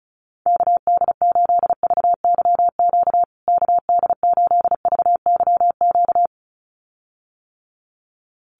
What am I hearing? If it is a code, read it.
Morse code, heard as a keyed tone, about 34 wpm, 700 Hz. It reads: KD8VYQ KD8VYQ
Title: USA Call Signs Repeated 2x 35wpm